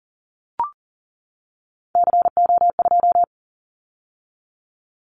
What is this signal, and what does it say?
Morse code, heard as a keyed tone, about 39 wpm, 700 Hz. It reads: CO2
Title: Callsign Prefixes for Top 100 DXCC - 40wpm